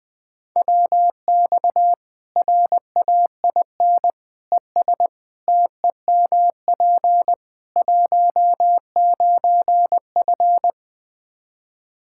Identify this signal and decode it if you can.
Morse code, heard as a keyed tone, about 20 wpm, 700 Hz. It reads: WX RAIN ES TEMP 19F
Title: Practice QSO - Exchange 2 - Info - 20wpm